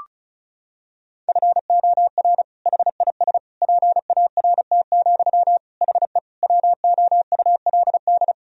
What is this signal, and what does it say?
Morse code, heard as a keyed tone, about 34 wpm, 700 Hz. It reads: FOR HIS PART, HE WOULD
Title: Sentences from Top 1000 Words - Review Part 2 - 35wpm